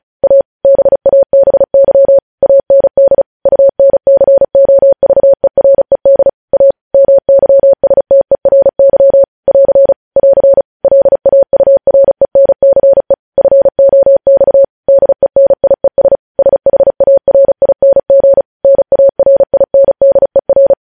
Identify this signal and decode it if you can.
Morse code, heard as a keyed tone, about 36 wpm, 550 Hz. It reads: A BABY AND UNCOVERED A MYSTERY <AR> <AR> LAURENCE FOX DENIES SHARING NARINDER
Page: News headlines in Morse code at 35 wpm